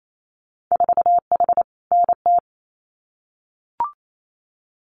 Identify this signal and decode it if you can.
Morse code, heard as a keyed tone, about 28 wpm, 700 Hz. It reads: 4H NT
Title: Winter Field Day Exchanges Encore 28wpm